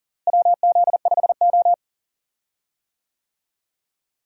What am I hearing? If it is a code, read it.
Morse code, heard as a keyed tone, about 40 wpm, 700 Hz. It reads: WZ5O
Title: USA Call Signs - 4 Characters - Repeated 3x - 40wpm